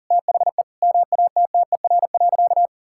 Morse code, heard as a keyed tone, about 37 wpm, 700 Hz. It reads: THE MATTER.
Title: Sentences from Top 1600 Words 40wpm